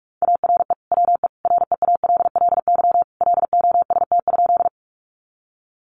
Morse code, heard as a keyed tone, about 44 wpm, 700 Hz. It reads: ARE WE REALLY LOST?
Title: Sentences from Top 1400 Words 45wpm